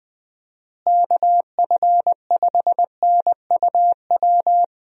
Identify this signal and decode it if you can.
Morse code, heard as a keyed tone, about 20 wpm, 700 Hz. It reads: KF5NUW
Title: USA Call Signs - 6 Characters - Repeated 2x - 20wpm